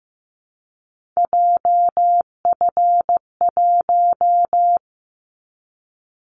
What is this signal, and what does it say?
Morse code, heard as a keyed tone, about 15 wpm, 700 Hz. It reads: JF1
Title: Callsign Prefixes for Top 25 DXCC - 15wpm